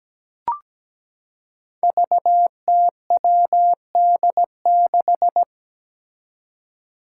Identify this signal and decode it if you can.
Morse code, heard as a keyed tone, about 17 wpm, 700 Hz. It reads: VTWD6
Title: Five Challenging Letter-Number Combinations 17wpm